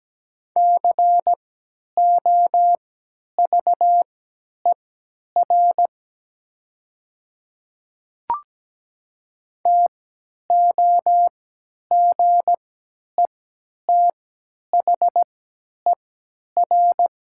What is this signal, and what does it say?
Morse code, heard as a keyed tone, about 17 wpm, spaced as for 10 wpm, 700 Hz. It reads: COVER TOGETHER
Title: 3x Character Spacing - Top 300 Words - 17wpm